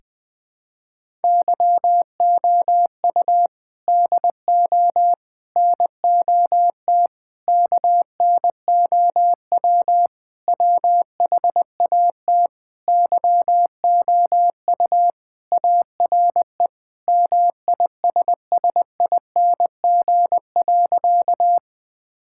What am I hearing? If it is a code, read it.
Morse code, heard as a keyed tone, about 20 wpm, 700 Hz. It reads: YOU DO NOT KNOW WHAT YOU ARE MISSING.
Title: Sentences with 5-Word Phrases - 20wpm